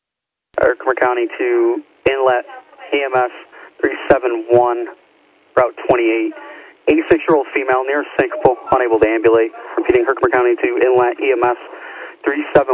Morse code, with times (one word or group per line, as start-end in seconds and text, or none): none